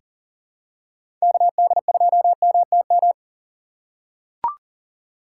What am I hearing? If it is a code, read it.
Morse code, heard as a keyed tone, about 40 wpm, 700 Hz. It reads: KD2MTM
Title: USA Call Signs - 6 Characters - 2x Character Spacing - 40wpm